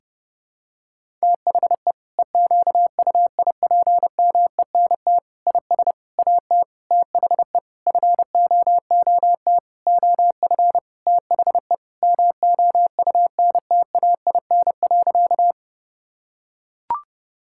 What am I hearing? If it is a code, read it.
Morse code, heard as a keyed tone, about 30 wpm, 700 Hz. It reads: THE EQUIPMENT IS AT THE FOOT OF THE MOUNTAIN.
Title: Sentences from Top 1000 Words - Review Part 5 - 30wpm